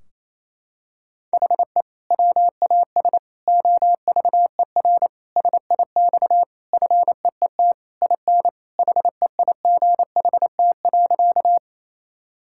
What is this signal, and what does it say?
Morse code, heard as a keyed tone, about 28 wpm, 700 Hz. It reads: HE WAS OVER SIX FEET IN HEIGHT.
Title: Sentences from Top 1600 Words 28wpm